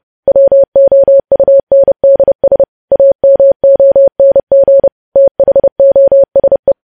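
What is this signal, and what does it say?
Morse code, heard as a keyed tone, about 30 wpm, 550 Hz. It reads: WOUNDS AMONG THOSE
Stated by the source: Morse code practice of news headlines at 30 wpm, preceded by vvv.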